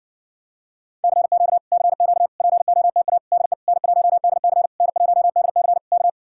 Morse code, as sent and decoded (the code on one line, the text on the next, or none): -.-. --.- / -.-. --.- / .--. --- - .- / -.. / -. .---- -.. -.-. / -. .---- -.. -.-. / -.-
CQ CQ POTA DE N1DC N1DC K